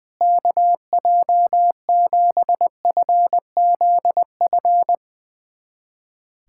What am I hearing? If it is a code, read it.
Morse code, heard as a keyed tone, about 20 wpm, 700 Hz. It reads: KJ7FZF
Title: USA Callsign DE Callsign 20wpm